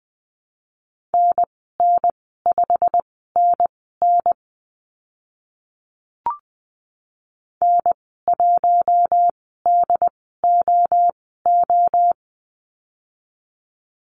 Morse code, heard as a keyed tone, about 20 wpm, spaced as for 15 wpm, 700 Hz. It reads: NN5NN N1DOO
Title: 088 Lesson 15g - Number 5 - Call Signs 20wpm